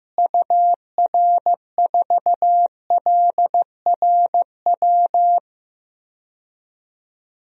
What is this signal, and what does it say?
Morse code, heard as a keyed tone, about 15 wpm, 700 Hz. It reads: UR4LRW
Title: DXCC-100 Call Signs - 6 Characters - Repeated 3x - 15wpm